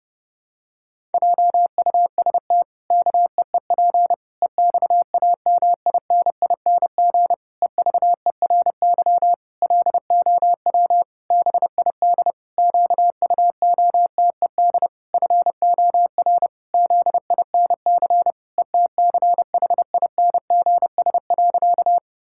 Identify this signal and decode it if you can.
Morse code, heard as a keyed tone, about 30 wpm, 700 Hz. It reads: JUST KEEP EXAMINING EVERY LOW BID QUOTED FOR ZINC ETCHINGS.
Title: Pangram Challenge v2 - 30wpm